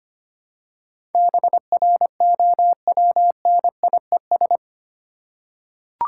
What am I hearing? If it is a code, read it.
Morse code, heard as a keyed tone, about 25 wpm, 700 Hz. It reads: BROWNIES
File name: practice / 25wpm / Difficult Words with Hints - 25wpm